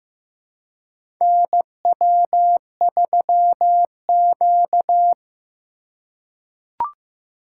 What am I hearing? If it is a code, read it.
Morse code, heard as a keyed tone, about 15 wpm, 700 Hz. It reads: NW3Q
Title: USA Call Signs - 4 Characters - 2x Character Spacing - 15wpm